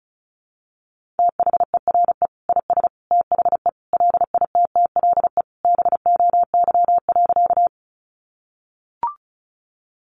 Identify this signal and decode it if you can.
Morse code, heard as a keyed tone, about 34 wpm, 700 Hz. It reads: THERE IS THE LITTLE BOY.
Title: Sentences from Top 200 Words 35wpm